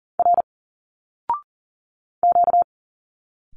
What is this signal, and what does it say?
Morse code, heard as a keyed tone, about 40 wpm, 700 Hz. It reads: R Q
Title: Single Letters 40wpm